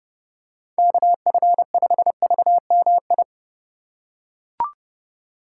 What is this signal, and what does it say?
Morse code, heard as a keyed tone, about 30 wpm, 700 Hz. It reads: KF5VMI
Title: USA Call Signs - 6 Characters - 4x Character Spacing - 30wpm